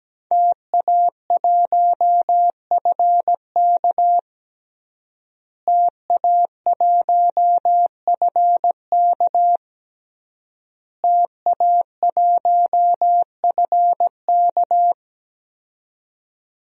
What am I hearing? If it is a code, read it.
Morse code, heard as a keyed tone, about 17 wpm, 700 Hz. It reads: TA1FK TA1FK TA1FK
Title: DXCC-100 Call Signs - 5 Characters - Repeated 3x - 17wpm